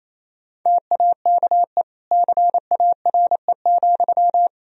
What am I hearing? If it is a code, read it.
Morse code, heard as a keyed tone, about 28 wpm, 700 Hz. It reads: TAKE CARE,
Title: Sentences from Top 1000 Words - Review Part 4 - 28wpm